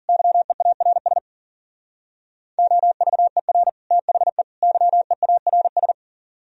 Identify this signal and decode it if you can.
Morse code, heard as a keyed tone, about 51 wpm, 700 Hz. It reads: YEARS OVER THE YEARS